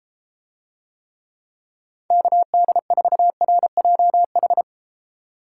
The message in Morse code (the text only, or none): KD4RJH